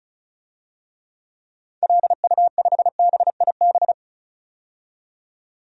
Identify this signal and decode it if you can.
Morse code, heard as a keyed tone, about 34 wpm, 700 Hz. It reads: LU5BIB
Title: DXCC-100 Call Signs - 6 Characters - 35wpm